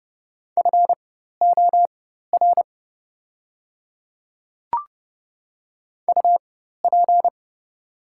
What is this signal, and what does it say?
Morse code, heard as a keyed tone, about 30 wpm, spaced as for 14 wpm, 700 Hz. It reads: FOR UP
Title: 4x Character Spacing - Top 100 Words - 30wpm